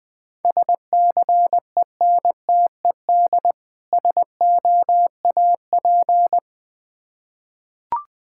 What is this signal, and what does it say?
Morse code, heard as a keyed tone, about 20 wpm, 690 Hz. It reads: SCENTED SOAP
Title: Difficult Words with Hints - 20wpm